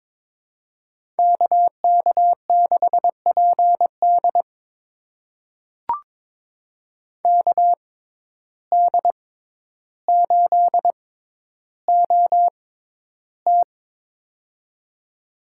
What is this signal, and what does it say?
Morse code, heard as a keyed tone, about 22 wpm, spaced as for 11 wpm, 700 Hz. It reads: KK6PD KD8OT
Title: USA Call Signs - 5 Characters - 6x Character Spacing - 22wpm